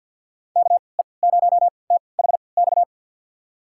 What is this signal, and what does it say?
Morse code, heard as a keyed tone, about 49 wpm, spaced as for 28 wpm, 700 Hz. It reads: KE0THX
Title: USA Call Signs - 3x Character Spacing - 50wpm